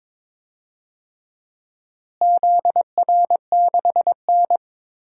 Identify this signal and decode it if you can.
Morse code, heard as a keyed tone, about 22 wpm, 700 Hz. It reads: ZR6N
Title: DXCC-100 Call Signs - 4 Characters - Repeated 3x - 22wpm